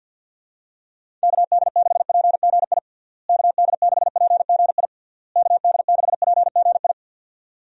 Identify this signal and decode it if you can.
Morse code, heard as a keyed tone, about 50 wpm, 700 Hz. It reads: KD6PGI KD6PGI KD6PGI
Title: USA Call Signs - 6 Characters - Repeated 3x - 50wpm